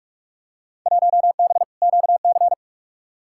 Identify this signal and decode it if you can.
Morse code, heard as a keyed tone, about 46 wpm, 700 Hz. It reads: 1B QC
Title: Field Day Exchanges Encore - 45wpm